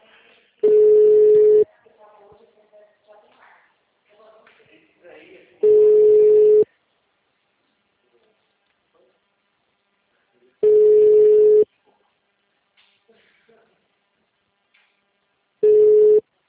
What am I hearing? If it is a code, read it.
Ringback tone (Europe, 425 Hz), with speech.